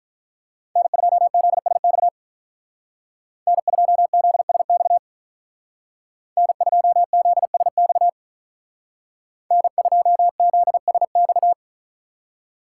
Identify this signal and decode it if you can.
Morse code, heard as a keyed tone, about 43 wpm, 700 Hz. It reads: N2ZSX N2ZSX N2ZSX N2ZSX
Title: Speed Racing - USA 1x3 Call Signs 35wpm